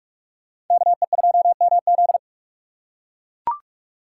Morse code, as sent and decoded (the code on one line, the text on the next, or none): -.- . ..--- -- --..
KE2MZ